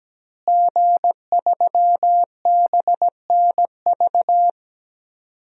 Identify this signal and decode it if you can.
Morse code, heard as a keyed tone, about 17 wpm, 700 Hz. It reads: G3BNV
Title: DXCC-100 Call Signs - 5 Characters - Repeated 3x - 17wpm